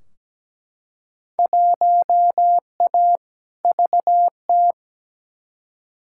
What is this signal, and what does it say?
Morse code, heard as a keyed tone, about 17 wpm, 700 Hz. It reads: 1A VT
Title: Field Day Exchanges Encore - 17wpm